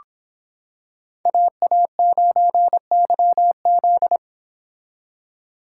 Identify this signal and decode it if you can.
Morse code, heard as a keyed tone, about 25 wpm, 700 Hz. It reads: AA9YZ